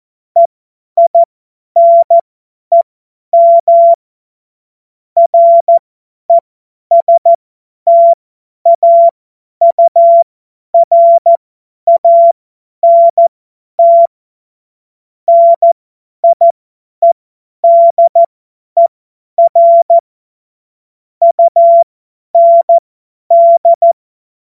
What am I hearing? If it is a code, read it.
Morse code, heard as a keyed tone, about 14 wpm, spaced as for 10 wpm, 680 Hz. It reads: EINEM RESTAURANT NIEDER UND
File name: Morsekurs / Witze / Witz 07. September 2021, 50 F70